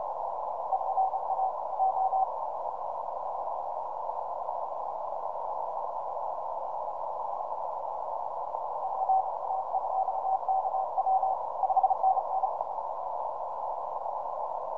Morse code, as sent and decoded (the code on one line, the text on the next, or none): none